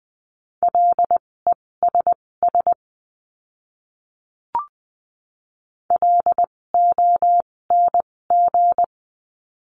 Morse code, heard as a keyed tone, about 20 wpm, spaced as for 16 wpm, 700 Hz. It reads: LESS LONG
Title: 256 Lesson 42a - Words 12-20wpm